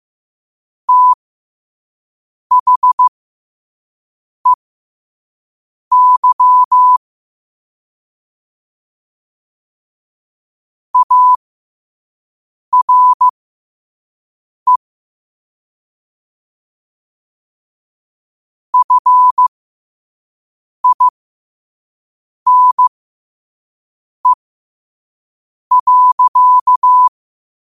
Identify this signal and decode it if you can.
Morse code, heard as a keyed tone, about 15 wpm, spaced as for 5 wpm, 1000 Hz. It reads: THEY ARE FINE.
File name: qotdfiles / 5wpm